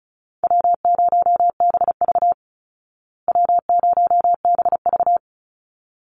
Morse code, heard as a keyed tone, about 34 wpm, 700 Hz. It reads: W0BV W0BV
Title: Easy Call Signs Repeated 2x - 35wpm